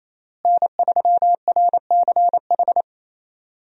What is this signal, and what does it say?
Morse code, heard as a keyed tone, about 28 wpm, 700 Hz. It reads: N3RCH
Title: USA Call Signs - 5 Characters - Repeated 3x - 28wpm